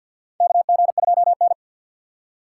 Morse code, heard as a keyed tone, about 48 wpm, 700 Hz. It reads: KG2N
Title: USA Call Signs - 4x Character Spacing - 50wpm